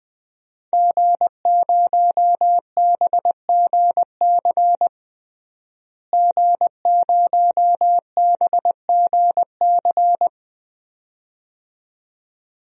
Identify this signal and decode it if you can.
Morse code, heard as a keyed tone, about 20 wpm, 700 Hz. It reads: G0BGC G0BGC
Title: DXCC-100 Call Signs - 5 Characters - Repeated 2x - 20wpm